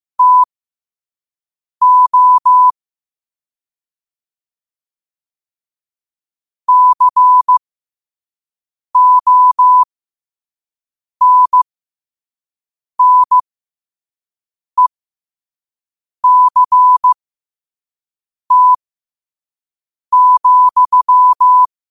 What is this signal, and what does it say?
Morse code, heard as a keyed tone, about 15 wpm, spaced as for 5 wpm, 1000 Hz. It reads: TO CONNECT,
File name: 5wpm